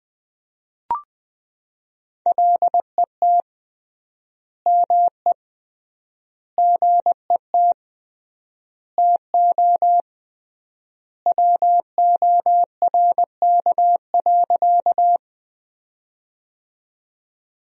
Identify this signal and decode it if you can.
Morse code, heard as a keyed tone, about 20 wpm, 700 Hz. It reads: LET ME GET TO WORK.
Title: 3x Word-Spacing - Sentences from Top 300 Words - 20wpm